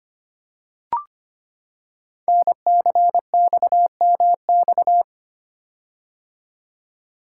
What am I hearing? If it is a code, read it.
Morse code, heard as a keyed tone, about 25 wpm, 700 Hz. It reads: NCXMX
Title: Code Groups - 5 Letters - 25wpm